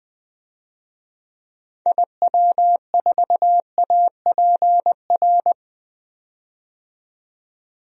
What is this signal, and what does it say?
Morse code, heard as a keyed tone, about 20 wpm, 700 Hz. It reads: IW4APR
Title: DXCC-100 Call Signs - 6 Characters - 20wpm